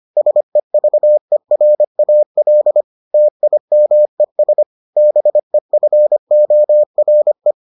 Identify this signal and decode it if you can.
Morse code, heard as a keyed tone, about 25 wpm, 600 Hz. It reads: SEVERAL TIMES BEFORE